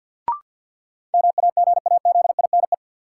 Morse code, heard as a keyed tone, about 53 wpm, 700 Hz. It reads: MAGAZINE
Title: Sentences from Top 1000 Words - Review Part 7 - New Word Drill - 50wpm